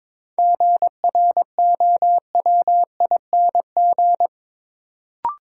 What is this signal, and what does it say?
Morse code, heard as a keyed tone, about 22 wpm, 700 Hz. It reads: GROWING
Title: Sets of 5 Words Encore - New Word Drill - 22wpm